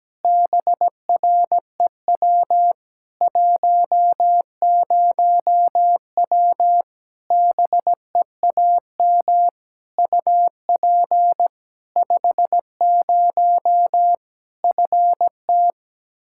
Morse code, heard as a keyed tone, about 17 wpm, 700 Hz. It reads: BREW 10W BEAM UP 50 FT
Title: Practice QSO - Exchange 2 - Info - 17wpm